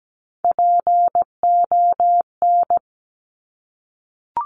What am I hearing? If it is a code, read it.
Morse code, heard as a keyed tone, about 17 wpm, 700 Hz. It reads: PON
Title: Top Three Letter Combinations in Words 17wpm